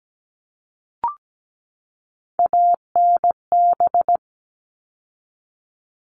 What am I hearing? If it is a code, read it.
Morse code, heard as a keyed tone, about 17 wpm, 700 Hz. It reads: ANB
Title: All Cut Numbers - 3-Digit 17wpm